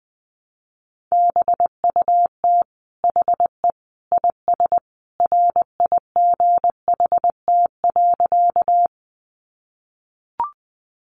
Morse code, heard as a keyed tone, about 20 wpm, 700 Hz. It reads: BUT HE IS RIGHT.
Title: Sentences from Top 200 Words 20wpm